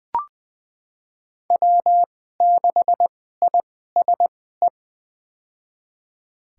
Morse code, heard as a keyed tone, about 20 wpm, spaced as for 14 wpm, 700 Hz. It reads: W6ISE
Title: USA Call Signs - 5 Characters - 2x Character Spacing - 20wpm